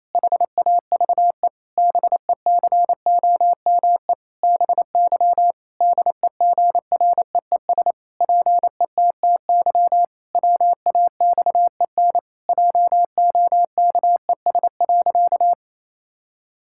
Morse code, heard as a keyed tone, about 28 wpm, 700 Hz. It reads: HAVE BECOME BY DEGREES PETTY WAXEN JOKES.
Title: Pangram Challenge v2 - 28wpm